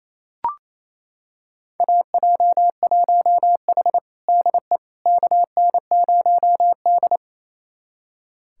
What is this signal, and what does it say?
Morse code, heard as a keyed tone, about 27 wpm, 700 Hz. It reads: AJ1H DE KN0D
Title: USA Callsign DE Callsign - 4 Characters - 28wpm